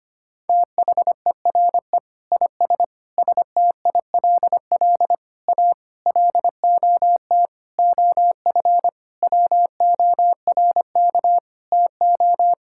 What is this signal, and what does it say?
Morse code, heard as a keyed tone, about 25 wpm, 700 Hz. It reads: THERE IS STILL A LOT OF WORK TO
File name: practice / 25wpm / Sentences with 9-Word Phrases - 25wpm